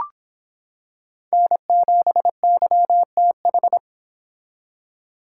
Morse code, heard as a keyed tone, about 25 wpm, 700 Hz. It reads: N7YTH